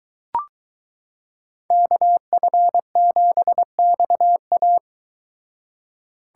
Morse code, heard as a keyed tone, about 22 wpm, 700 Hz. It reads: KF7XA